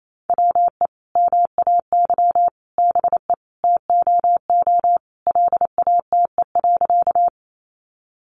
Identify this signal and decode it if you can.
Morse code, heard as a keyed tone, about 28 wpm, 700 Hz. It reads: WE MAY BE TOO LATE.
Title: Sentences from Top 300 Words 28wpm